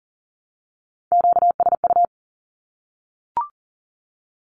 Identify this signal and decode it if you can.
Morse code, heard as a keyed tone, about 40 wpm, 700 Hz. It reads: QSU
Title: Q-Codes Repeated 3x 40wpm